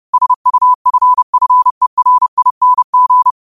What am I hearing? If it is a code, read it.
Morse code, heard as a keyed tone, about 30 wpm, 1000 Hz. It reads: SUFFERING
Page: Quotes for Mon, 28 Apr 2025 in Morse Code at 30 words per minute.